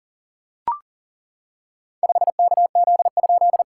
Morse code, heard as a keyed tone, about 40 wpm, 700 Hz. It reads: 5KZ?
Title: SOTA QSO Practice 40wpm